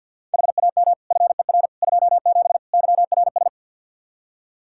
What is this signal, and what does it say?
Morse code, heard as a keyed tone, about 52 wpm, 700 Hz. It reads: HAM FER 27 YRS
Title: Practice QSO - Exchange 3 - More Info - 50wpm